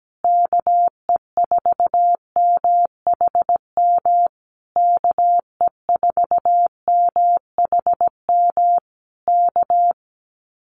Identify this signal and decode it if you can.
Morse code, heard as a keyed tone, about 17 wpm, 700 Hz. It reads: KE4MHM KE4MHM K
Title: Common QSO Phrases Encore 17wpm